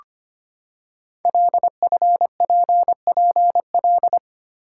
Morse code, heard as a keyed tone, about 25 wpm, 700 Hz. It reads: LFPPL
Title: Code Groups - 5 Letters - 25wpm